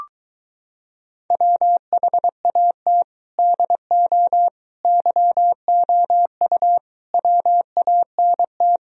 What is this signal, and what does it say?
Morse code, heard as a keyed tone, about 24 wpm, 700 Hz. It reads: WHAT DO YOU WANT